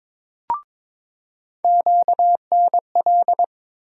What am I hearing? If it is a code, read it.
Morse code, heard as a keyed tone, about 22 wpm, 700 Hz. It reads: QNL
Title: USA Call Sign Postfixes - 3 Letters 22wpm